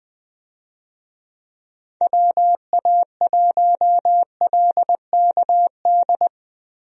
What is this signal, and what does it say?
Morse code, heard as a keyed tone, about 20 wpm, 700 Hz. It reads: WA1LKD
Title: USA Call Signs - 6 Characters - Repeated 3x - 20wpm